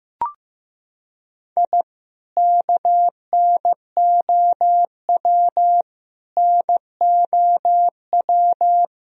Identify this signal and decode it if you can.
Morse code, heard as a keyed tone, about 15 wpm, 700 Hz. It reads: I KNOW NOW
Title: Sentences from Top 1000 Words - Review Part 1 - 15wpm